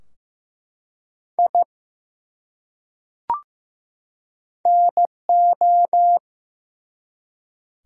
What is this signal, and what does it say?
Morse code, heard as a keyed tone, about 15 wpm, 700 Hz. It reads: I NO
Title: Top 100 Words 15wpm